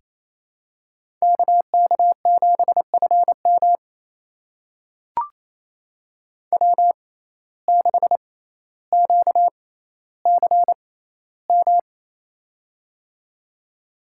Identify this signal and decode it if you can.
Morse code, heard as a keyed tone, about 28 wpm, spaced as for 14 wpm, 700 Hz. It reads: KK7FM W6QCM
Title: USA Call Signs - 5 Characters - 6x Character Spacing - 28wpm